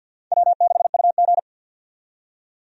Morse code, heard as a keyed tone, about 49 wpm, 700 Hz. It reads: W6UG
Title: USA Call Signs - 4 Characters - 50wpm